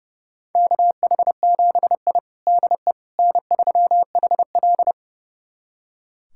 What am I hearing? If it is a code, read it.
Morse code, heard as a keyed tone, about 30 wpm, 700 Hz. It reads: KH7I DE N3HL
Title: USA Callsign DE Callsign - 4 Characters - 30wpm